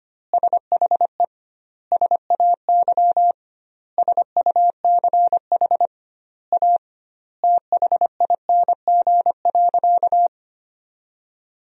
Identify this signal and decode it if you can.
Morse code, heard as a keyed tone, about 25 wpm, 700 Hz. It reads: SHE SAY SUCH A THING.
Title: 2x Word-Spacing - Sentences from Top 200 Words - 25wpm